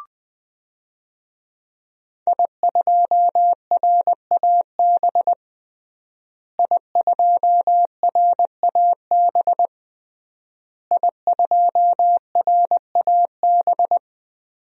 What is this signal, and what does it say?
Morse code, heard as a keyed tone, about 20 wpm, 700 Hz. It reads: I2RAB I2RAB I2RAB
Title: DXCC-100 Call Signs - 5 Characters - Repeated 3x - 20wpm